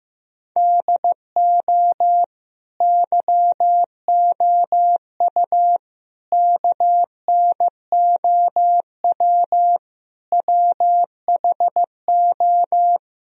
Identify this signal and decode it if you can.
Morse code, heard as a keyed tone, about 15 wpm, 700 Hz. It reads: DO YOU KNOW WHO
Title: Sentences from Top 1000 Words - Review Part 2 - 15wpm